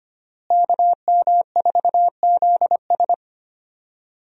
Morse code, heard as a keyed tone, about 25 wpm, 700 Hz. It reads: KM4ZS
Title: USA Call Signs - 5 Characters - Repeated 2x - 25wpm